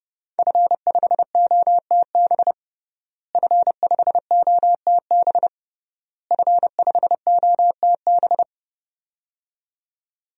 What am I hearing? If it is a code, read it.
Morse code, heard as a keyed tone, about 30 wpm, 700 Hz. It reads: F5OTB F5OTB F5OTB
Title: DXCC-100 Call Signs - 5 Characters - Repeated 3x - 30wpm